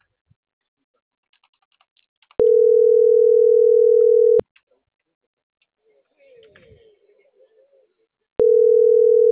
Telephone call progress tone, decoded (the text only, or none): ringback tone